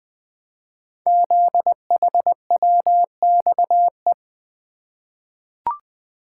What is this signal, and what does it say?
Morse code, heard as a keyed tone, about 20 wpm, 700 Hz. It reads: ZHWXE
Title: Code Groups - 5 Letters - 20wpm